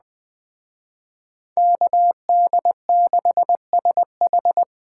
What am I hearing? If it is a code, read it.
Morse code, heard as a keyed tone, about 20 wpm, 700 Hz. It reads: KD6SH